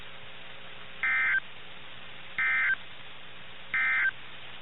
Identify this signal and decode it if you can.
Emergency Alert System (SAME) tones, heard as harsh buzzing bursts. It is heard from 0.9 to 4.2 s.
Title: NOAA Weather Radio - KIH61